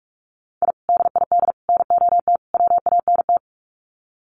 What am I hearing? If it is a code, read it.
Morse code, heard as a keyed tone, about 44 wpm, 700 Hz. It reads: I DID NOT WANT
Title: Sets of 4 Words 45wpm